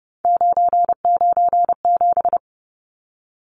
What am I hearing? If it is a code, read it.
Morse code, heard as a keyed tone, about 30 wpm, 700 Hz. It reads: 997
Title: Numbers - 3-Digit 30wpm